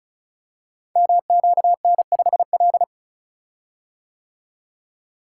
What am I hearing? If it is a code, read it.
Morse code, heard as a keyed tone, about 36 wpm, 700 Hz. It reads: MQN5L
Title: Code Groups - 5 Letter:Numbers - 35wpm